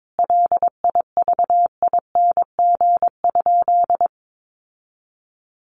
Morse code, heard as a keyed tone, about 22 wpm, 700 Hz. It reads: LIVING?
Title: Sentences from Top 1100 Words 22wpm